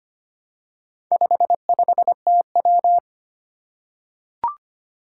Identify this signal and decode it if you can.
Morse code, heard as a keyed tone, about 25 wpm, 700 Hz. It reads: 55TW
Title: Four Challenging Letter-Number Combinations 25wpm